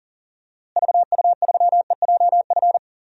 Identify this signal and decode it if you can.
Morse code, heard as a keyed tone, about 40 wpm, 700 Hz. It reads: VU3EJF
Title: DXCC-100 Call Signs - 6 Characters - Repeated 2x - 40wpm